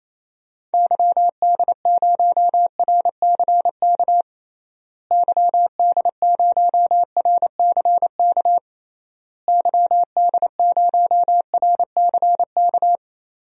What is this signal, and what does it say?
Morse code, heard as a keyed tone, about 28 wpm, 700 Hz. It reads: YD0RCK YD0RCK YD0RCK
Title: DXCC-100 Call Signs - 6 Characters - Repeated 3x - 28wpm